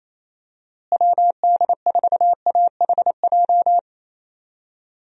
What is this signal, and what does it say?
Morse code, heard as a keyed tone, about 28 wpm, 700 Hz. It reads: WD4AHJ
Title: USA Call Signs - 6 Characters - Repeated 2x - 28wpm